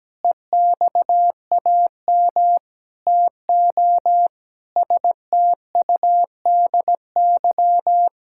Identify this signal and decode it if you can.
Morse code, heard as a keyed tone, about 17 wpm, 700 Hz. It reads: EXAM TO STUDY
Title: Sentences from Top 1000 Words - Review Part 5 - 17wpm